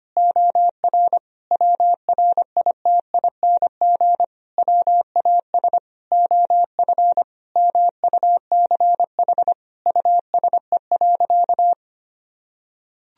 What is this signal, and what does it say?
Morse code, heard as a keyed tone, about 25 wpm, 700 Hz. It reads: OR WRITING WAS OF MUCH USE.
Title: Sentences from Top 1000 Words - Review Part 3 - 25wpm